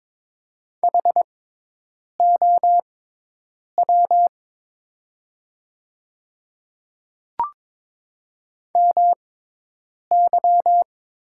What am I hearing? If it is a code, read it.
Morse code, heard as a keyed tone, about 23 wpm, spaced as for 8 wpm, 700 Hz. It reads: HOW MY
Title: 6x Character Spacing - Top 100 Words - 22wpm